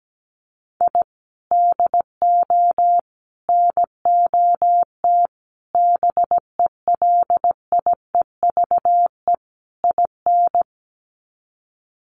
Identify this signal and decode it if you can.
Morse code, heard as a keyed tone, about 17 wpm, 700 Hz. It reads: I DO NOT BELIEVE IN
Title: Sets of 5 Words Encore 17wpm